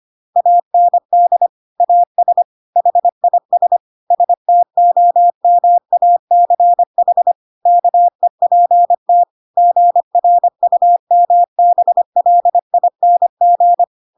Morse code, heard as a keyed tone, about 25 wpm, 700 Hz. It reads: AND AS HIS STOMACH KEPT GRUMBLING
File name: Pinocchio / Pinocchio ch5 25 25 ik1hge